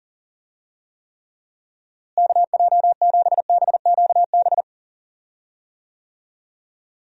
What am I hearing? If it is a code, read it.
Morse code, heard as a keyed tone, about 40 wpm, 700 Hz. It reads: KJ7BQB
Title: USA Call Signs - 6 Characters - Repeated 3x - 40wpm